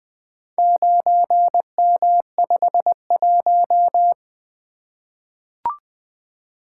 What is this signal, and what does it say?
Morse code, heard as a keyed tone, about 20 wpm, 700 Hz. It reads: 9M51
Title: Four Letter-Number Combinations - 20wpm